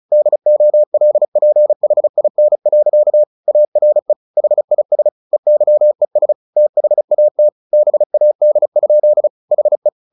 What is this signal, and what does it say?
Morse code, heard as a keyed tone, about 34 wpm, 600 Hz. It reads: DOLPHIN. ARE HIS EYES THAT BAD? HE
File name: the old man and the sea / the old man and the sea 1 35wpm